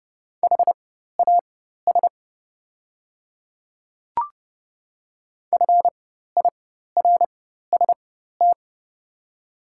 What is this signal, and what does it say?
Morse code, heard as a keyed tone, about 31 wpm, spaced as for 14 wpm, 700 Hz. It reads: HAS FIRST
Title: 4x Character Spacing - Top 100 Words - 30wpm